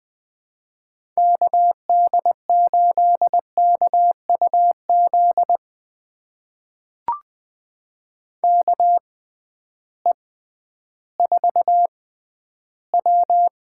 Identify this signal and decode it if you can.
Morse code, heard as a keyed tone, about 20 wpm, spaced as for 12 wpm, 700 Hz. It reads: KD8KUZ KE4W
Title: USA Call Signs - 6 Characters - 6x Character Spacing - 20wpm